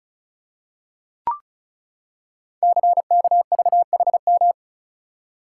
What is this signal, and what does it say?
Morse code, heard as a keyed tone, about 36 wpm, 690 Hz. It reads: CKVHM
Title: Code Groups - 5 Letters - 35wpm